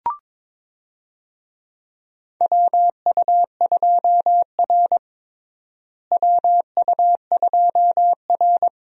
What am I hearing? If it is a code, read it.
Morse code, heard as a keyed tone, about 22 wpm, 700 Hz. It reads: WU2R WU2R
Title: USA Call Signs - 4 Characters - Repeated 2x - 22wpm